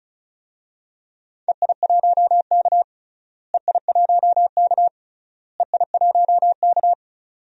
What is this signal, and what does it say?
Morse code, heard as a keyed tone, about 36 wpm, 700 Hz. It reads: EI1K EI1K EI1K
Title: DXCC-100 Call Signs - 4 Characters - Repeated 3x - 35wpm